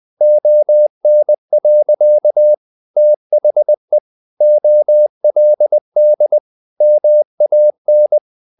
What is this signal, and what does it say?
Morse code, heard as a keyed tone, about 20 wpm, 600 Hz. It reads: ON. THE OLD MAN
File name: the old man and the sea / the old man and the sea part 1 20wpm